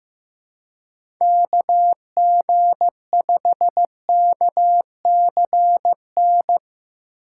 Morse code, heard as a keyed tone, about 17 wpm, 700 Hz. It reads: KG5KCN
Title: Speed Racing - USA 2x3 Call Signs 15wpm